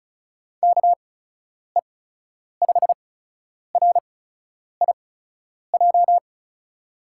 Morse code, heard as a keyed tone, about 34 wpm, spaced as for 10 wpm, 700 Hz. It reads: KE5RIJ
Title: USA Call Signs - 6 Characters - 8x Character Spacing - 35wpm